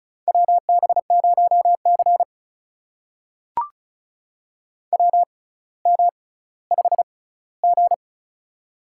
Morse code, heard as a keyed tone, about 36 wpm, spaced as for 18 wpm, 700 Hz. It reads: WB0C WM5G
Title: USA Call Signs - 4 Characters - 6x Character Spacing - 35wpm